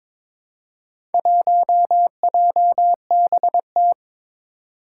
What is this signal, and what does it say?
Morse code, heard as a keyed tone, about 22 wpm, 700 Hz. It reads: 1JBT
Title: Four Challenging Letter-Number Combinations 22wpm